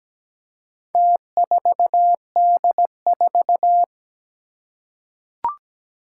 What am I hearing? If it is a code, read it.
Morse code, heard as a keyed tone, about 17 wpm, 700 Hz. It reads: T4D4
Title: Four Challenging Letter-Number Combinations 17wpm